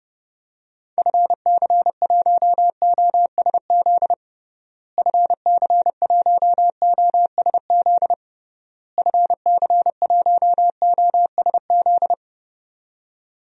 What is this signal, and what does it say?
Morse code, heard as a keyed tone, about 30 wpm, 700 Hz. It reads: FC1OSZ FC1OSZ FC1OSZ
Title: DXCC-100 Call Signs - 6 Characters - Repeated 3x - 30wpm